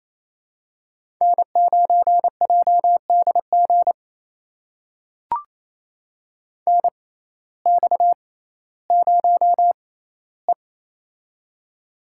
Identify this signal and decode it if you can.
Morse code, heard as a keyed tone, about 28 wpm, spaced as for 15 wpm, 700 Hz. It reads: N9JDG NX0E
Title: USA Call Signs - 6x Character Spacing - 28wpm